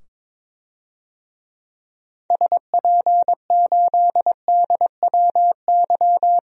Morse code, heard as a keyed tone, about 22 wpm, 700 Hz. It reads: SP8DWY
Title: DXCC-100 Call Signs - 6 Characters - 22wpm